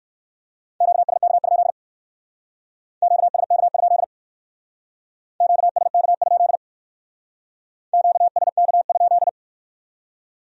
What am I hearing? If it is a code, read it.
Morse code, heard as a keyed tone, about 56 wpm, 700 Hz. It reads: QSK? QSK? QSK? QSK?
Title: Speed-Racing Q-Codes - 45wpm